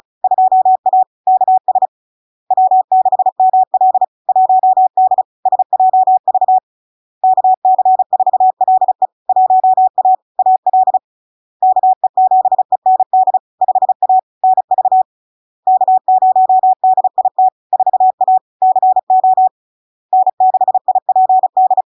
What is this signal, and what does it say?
Morse code, heard as a keyed tone, about 36 wpm, 750 Hz. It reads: 2A KS W6ML 1D SJV KC4LE 1A AL KE7END 5A NV K0DIT 4A CO N6IPD
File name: LARK-FD-2022-Log-in-CW-35wpm-750-Hz